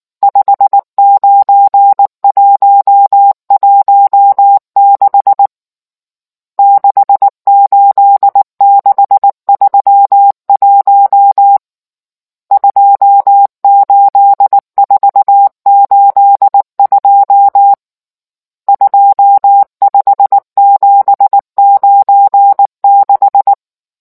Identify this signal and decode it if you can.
Morse code, heard as a keyed tone, about 19 wpm, 800 Hz. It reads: 59116 68631 28482 25796